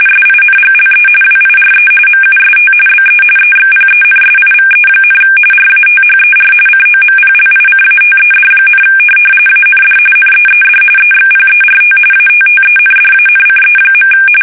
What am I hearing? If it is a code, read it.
RTTY, about 45.45 baud. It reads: SINGLE WIRE MUST BE EXPENSIVE, RATHER COMPLICATED, AND NEED EXPERT ATTENDANCE.
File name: fsk850